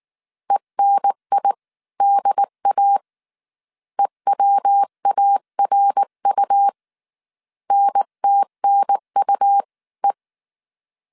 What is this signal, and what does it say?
Morse code, heard as a keyed tone, about 19 wpm, spaced as for 17 wpm, 800 Hz. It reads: ENIDA EWARU NTNUE